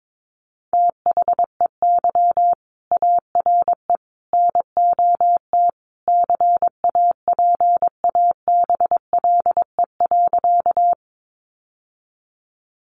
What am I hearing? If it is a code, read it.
Morse code, heard as a keyed tone, about 22 wpm, 700 Hz. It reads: THEY ARE NOT CAPABLE.
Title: Sentences from Top 1200 Words 22wpm